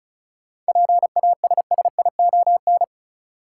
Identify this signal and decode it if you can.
Morse code, heard as a keyed tone, about 36 wpm, 700 Hz. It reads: PASSION
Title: Sentences from Top 1300 Words - New Word Drill - 35wpm